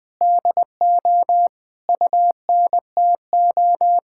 Morse code, heard as a keyed tone, about 20 wpm, 700 Hz. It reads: DO UNTO
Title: Sentences with 8-Word Phrases - 20wpm